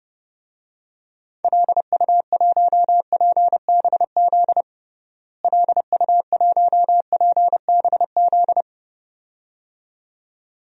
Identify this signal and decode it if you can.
Morse code, heard as a keyed tone, about 30 wpm, 700 Hz. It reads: LU1PBZ LU1PBZ
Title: DXCC-100 Call Signs - 6 Characters - Repeated 2x - 30wpm